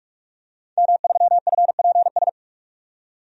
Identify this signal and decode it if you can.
Morse code, heard as a keyed tone, about 44 wpm, 700 Hz. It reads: M3FPS
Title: DXCC-100 Call Signs - 5 Characters - 45wpm